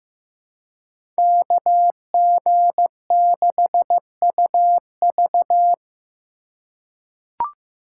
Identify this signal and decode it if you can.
Morse code, heard as a keyed tone, about 15 wpm, 700 Hz. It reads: KG6UV
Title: USA Call Signs - 5 Characters - 4x Character Spacing - 15wpm